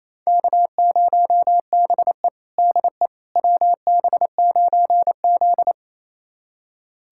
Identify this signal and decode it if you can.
Morse code, heard as a keyed tone, about 27 wpm, 690 Hz. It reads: K0BE DE WB9Z
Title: USA Callsign DE Callsign - 4 Characters - 28wpm